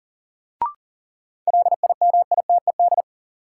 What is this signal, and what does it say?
Morse code, heard as a keyed tone, about 40 wpm, 700 Hz. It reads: LIMITED
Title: Sentences from Top 900 Words - New Word Drill - 40wpm